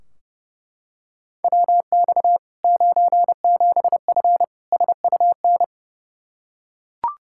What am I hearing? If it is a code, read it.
Morse code, heard as a keyed tone, about 30 wpm, 700 Hz. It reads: WX 97F SUN
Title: Practice QSO - Exchange 2 - Info - 30wpm